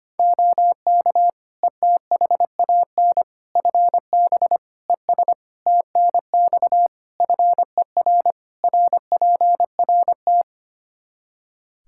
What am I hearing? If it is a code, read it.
Morse code, heard as a keyed tone, about 25 wpm, 700 Hz. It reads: OK ETHAN FB ES TNX FER RPRT
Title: Practice QSO - Exchange 2 - Info - 25wpm